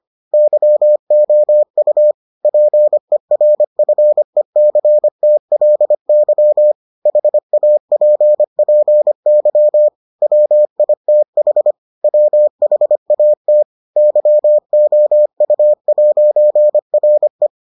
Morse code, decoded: YOU PERFECTLY HAPPY WITH WHAT YOU'RE

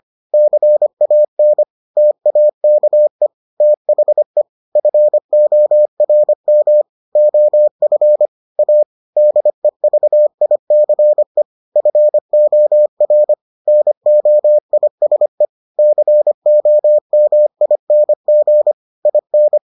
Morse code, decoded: CAN TAKE THE FORM OF A DEVICE FOR NOISE COMING IN

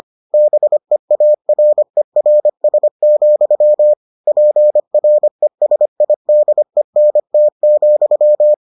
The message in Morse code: BEARERS, PRESIDENT,

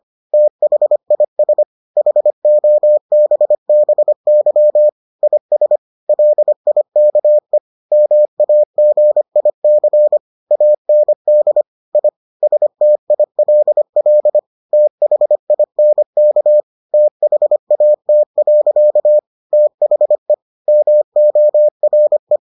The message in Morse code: THIS HOBBY IS LIKE MAGIC AND I STILL THINK THAT. THE MORE

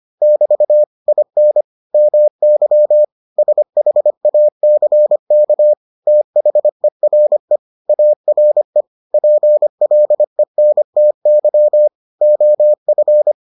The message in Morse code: <BT> IN MY SHACK THERE ARE PLENTY OF

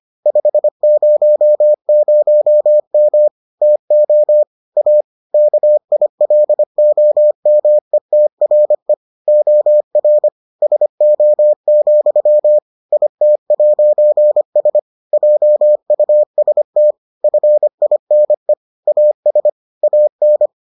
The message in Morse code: 500M TO A KILOMETRE OR SO, IT'S JUST FINE AS AN

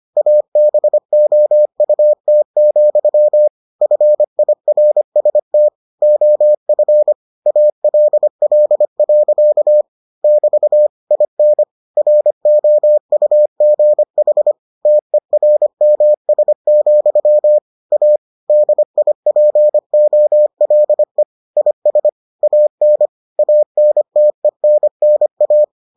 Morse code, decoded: ABOUT, FIRST OF ALL. <BT> IN ROUGH TERMS, A DIPOLE IS AN ANTENNA

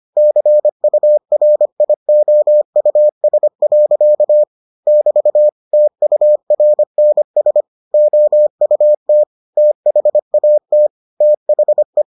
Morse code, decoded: CURIOUS. <BT> TURNS OUT THAT THE

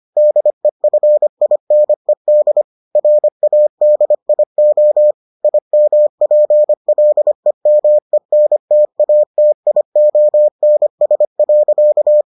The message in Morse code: DEFINED RADIO IMPLEMENTATIONS.

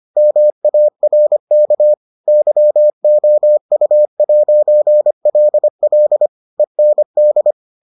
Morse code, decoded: MARK YOU'LL END

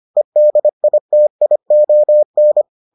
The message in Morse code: EDITION

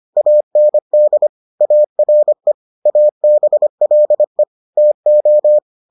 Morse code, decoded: AND ARE ABLE TO

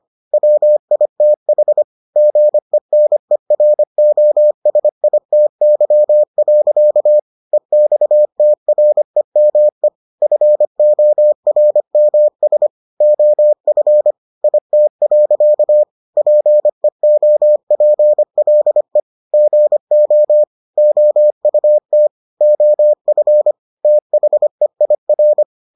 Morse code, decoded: WITH GENEROSITY. EXTREME FORMS OF IT. PEOPLE GO OUT OF THEIR

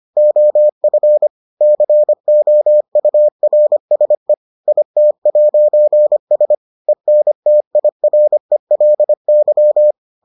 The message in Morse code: OF COURSE IT'S ENTIRELY

